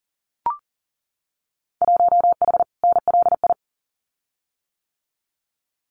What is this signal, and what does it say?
Morse code, heard as a keyed tone, about 40 wpm, 700 Hz. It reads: 1H NLI
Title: Winter Field Day Exchanges 40wpm